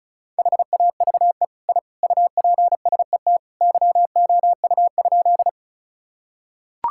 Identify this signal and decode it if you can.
Morse code, heard as a keyed tone, about 34 wpm, 700 Hz. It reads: HAVE I UPSET YOU?
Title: Sentences from Top 1000 Words - Review Part 5 - 35wpm